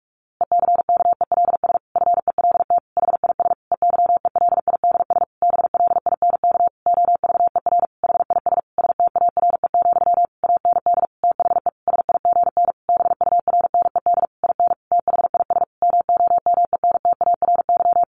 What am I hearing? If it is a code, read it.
Morse code, heard as a keyed tone, about 44 wpm, 700 Hz. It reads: ECKELS FELT HIS EYELIDS BLINK OVER HIS STARE, AND THE SIGN BURNED IN THIS MOMENTARY
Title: A Sound Of Thunder 45wpm